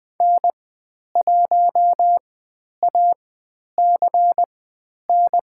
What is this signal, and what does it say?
Morse code, heard as a keyed tone, about 21 wpm, spaced as for 10 wpm, 700 Hz. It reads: N1ACN
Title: 109 Lesson 18g - Letter m - Call Signs 20wpm